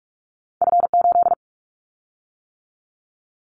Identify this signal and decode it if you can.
Morse code, heard as a keyed tone, about 44 wpm, 700 Hz. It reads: F8
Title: Callsign Prefixes for Top 25 DXCC - 45wpm